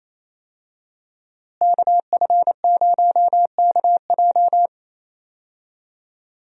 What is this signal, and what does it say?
Morse code, heard as a keyed tone, about 28 wpm, 700 Hz. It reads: KF0KJ
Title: USA Call Signs - 5 Characters - Repeated 2x - 28wpm